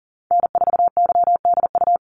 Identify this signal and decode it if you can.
Morse code, heard as a keyed tone, about 40 wpm, 700 Hz. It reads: N4YDU
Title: Call Signs Repeated 3x - 40wpm